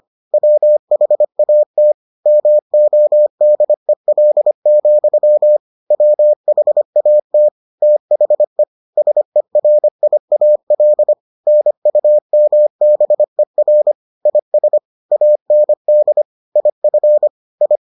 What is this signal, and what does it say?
Morse code, heard as a keyed tone, about 25 wpm, 600 Hz. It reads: WHAT MODEL, WHAT THE SERIAL NUMBER IS AND IF I